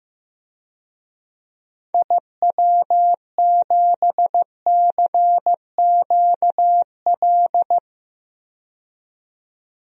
Morse code, heard as a keyed tone, about 15 wpm, 700 Hz. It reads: IW7CQL
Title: DXCC-100 Call Signs - 6 Characters - 15wpm